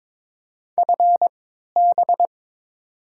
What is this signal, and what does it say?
Morse code, heard as a keyed tone, about 22 wpm, spaced as for 13 wpm, 700 Hz. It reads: FB
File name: Top 20 QSO Elements - 3x Character Spacing - 22wpm